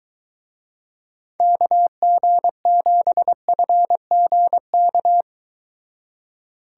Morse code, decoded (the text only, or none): KG7FGK